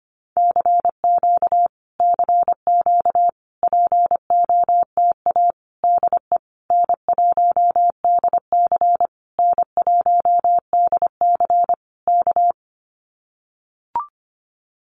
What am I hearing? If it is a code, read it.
Morse code, heard as a keyed tone, about 25 wpm, 700 Hz. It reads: CQ CQ POTA DE N1DC N1DC K
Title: Common QSO Phrases Encore 25wpm